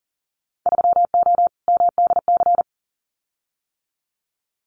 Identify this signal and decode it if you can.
Morse code, heard as a keyed tone, about 40 wpm, 700 Hz. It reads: 3O MDC
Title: Winter Field Day Exchanges Encore 40wpm